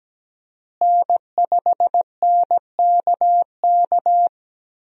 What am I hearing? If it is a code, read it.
Morse code, heard as a keyed tone, about 17 wpm, 700 Hz. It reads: N5NKK
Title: USA Call Signs - 5 Characters - Repeated 2x - 17wpm